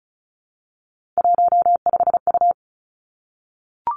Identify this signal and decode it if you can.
Morse code, heard as a keyed tone, about 36 wpm, 700 Hz. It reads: 15U
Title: Three Challenging Letter-Number Combinations 35wpm